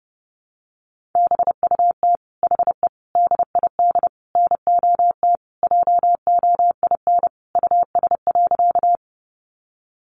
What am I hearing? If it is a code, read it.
Morse code, heard as a keyed tone, about 30 wpm, 700 Hz. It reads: BUT HE DID NOT JOIN US.
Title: Sentences from Top 1400 Words 30wpm